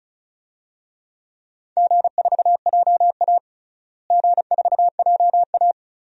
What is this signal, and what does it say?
Morse code, heard as a keyed tone, about 36 wpm, 700 Hz. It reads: G4JA G4JA
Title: DXCC-100 Call Signs - 4 Characters - Repeated 2x - 35wpm